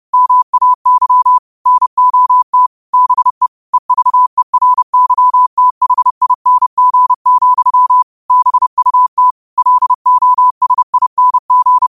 Morse code, heard as a keyed tone, about 30 wpm, 1000 Hz. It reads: MAY NOT BE EVERYTHING, BUT LOSING